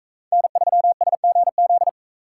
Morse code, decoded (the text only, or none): N3SGZ